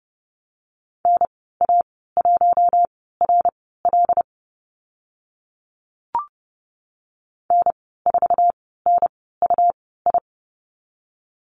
Morse code, heard as a keyed tone, about 30 wpm, spaced as for 17 wpm, 700 Hz. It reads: NA1RL N4NUI
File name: Lesson 19g - Letter w - Call Signs 30wpm